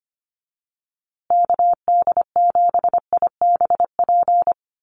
Morse code, heard as a keyed tone, about 25 wpm, 700 Hz. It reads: KD7IBP
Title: USA Call Signs Repeated 3x 25wpm